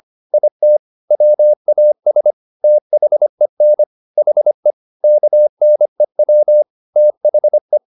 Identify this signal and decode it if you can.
Morse code, heard as a keyed tone, about 25 wpm, 600 Hz. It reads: IT WAS THEN HE KNEW THE